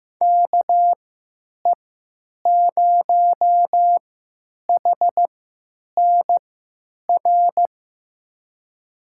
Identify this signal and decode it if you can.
Morse code, heard as a keyed tone, about 15 wpm, spaced as for 9 wpm, 700 Hz. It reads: KE0HNR
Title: USA Call Signs - 6 Characters - 3x Character Spacing - 15wpm